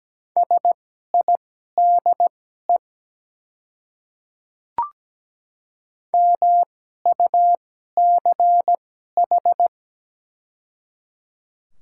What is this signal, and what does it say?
Morse code, heard as a keyed tone, about 17 wpm, spaced as for 12 wpm, 700 Hz. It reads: SIDE MUCH
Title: 2x Character Spacing - Top 200 Words - 17wpm